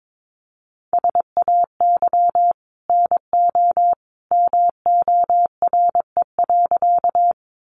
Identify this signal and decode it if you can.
Morse code, heard as a keyed tone, about 22 wpm, 700 Hz. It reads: SAY NO MORE.
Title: Sentences from Top 200 Words 22wpm